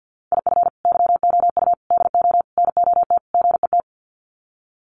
Morse code, heard as a keyed tone, about 48 wpm, 700 Hz. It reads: IF YOU DO NOT GET
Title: Sets of 5 Words Encore2 50wpm